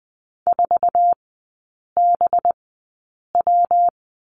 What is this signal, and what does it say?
Morse code, heard as a keyed tone, about 20 wpm, spaced as for 8 wpm, 700 Hz. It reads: 4BW
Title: 205 Lesson 33e - Letter j - Three Characters 20wpm